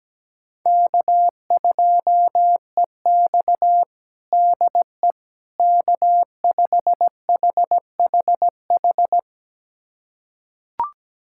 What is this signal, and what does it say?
Morse code, heard as a keyed tone, about 17 wpm, 700 Hz. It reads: K2EX DE K5HHH
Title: USA Callsign DE Callsign 17wpm